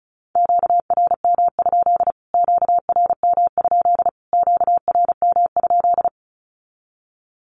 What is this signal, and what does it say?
Morse code, heard as a keyed tone, about 34 wpm, 700 Hz. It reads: QRM? QRM? QRM?
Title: Q-Codes Repeated 3x 35wpm